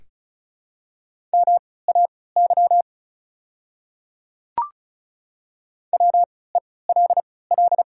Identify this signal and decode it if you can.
Morse code, heard as a keyed tone, about 36 wpm, spaced as for 20 wpm, 700 Hz. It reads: MAY WELL
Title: 3x Character Spacing - Top 200 Words - 35wpm